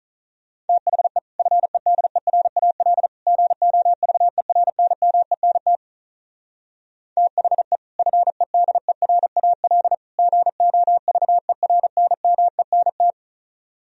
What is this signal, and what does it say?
Morse code, heard as a keyed tone, about 44 wpm, 700 Hz. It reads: THE FEDERAL GOVERNMENT THE FEDERAL GOVERNMENT